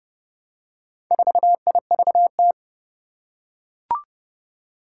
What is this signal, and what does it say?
Morse code, heard as a keyed tone, about 30 wpm, 700 Hz. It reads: 4IVT
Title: Four Challenging Letter-Number Combinations 30wpm